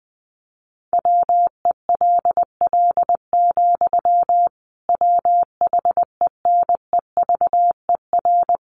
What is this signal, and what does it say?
Morse code, heard as a keyed tone, about 20 wpm, 700 Hz. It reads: WELL, WHENEVER
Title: Sentences from Top 1200 Words 20wpm